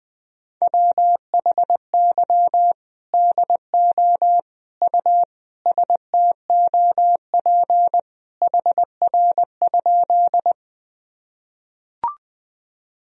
Morse code, heard as a keyed tone, about 21 wpm, 700 Hz. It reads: WHY DO U STOP HR?
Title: Sentences from Top 300 Words and Abbreviations 20wpm